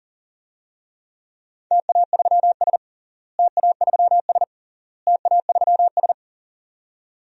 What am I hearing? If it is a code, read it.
Morse code, heard as a keyed tone, about 40 wpm, 700 Hz. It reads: TA3S TA3S TA3S
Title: DXCC-100 Call Signs - 4 Characters - Repeated 3x - 40wpm